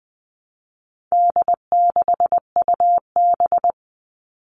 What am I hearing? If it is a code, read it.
Morse code, heard as a keyed tone, about 20 wpm, 700 Hz. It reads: D6UB
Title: All Cut Numbers - 4-Digit 20wpm